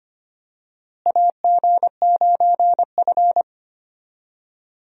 Morse code, heard as a keyed tone, about 25 wpm, 700 Hz. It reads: AG9F
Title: USA Call Signs - 4 Characters - Repeated 3x - 25wpm